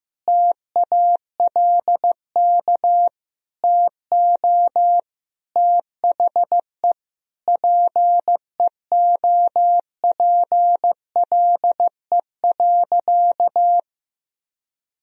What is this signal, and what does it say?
Morse code, heard as a keyed tone, about 14 wpm, 700 Hz. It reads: TALK TO THE PEOPLE.
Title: Sentences from Top 400 Words and Abbreviations 15wpm